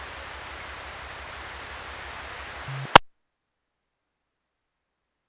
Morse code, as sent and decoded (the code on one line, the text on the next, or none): none